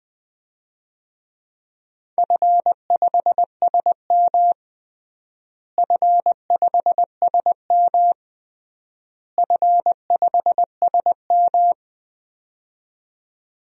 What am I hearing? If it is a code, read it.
Morse code, heard as a keyed tone, about 21 wpm, 700 Hz. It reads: F5SM F5SM F5SM
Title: DXCC-100 Call Signs - 4 Characters - Repeated 3x - 20wpm